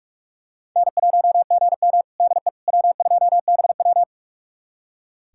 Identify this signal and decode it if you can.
Morse code, heard as a keyed tone, about 41 wpm, 700 Hz. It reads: N1GM DE W2BW
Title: USA Callsign DE Callsign - 4 Characters - 45wpm